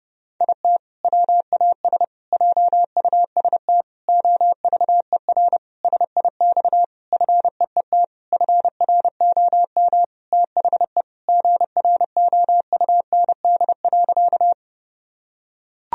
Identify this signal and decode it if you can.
Morse code, heard as a keyed tone, about 32 wpm, 700 Hz. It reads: IT WAS JUST OVER SIX FEET FROM THE GROUND.
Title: Sentences from Top 1000 Words - Review Part 2 - 30wpm